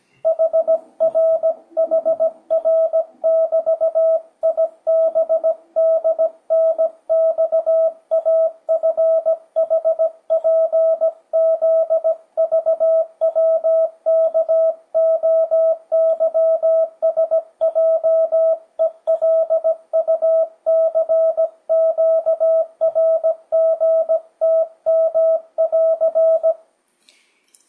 Morse code, heard as a keyed tone, about 17 wpm, 650 Hz. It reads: HRHR<BT>IBDNXAFHPZVWKOYSJELUCQRGTM<AR>
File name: 2025cw / speedup / 17wpm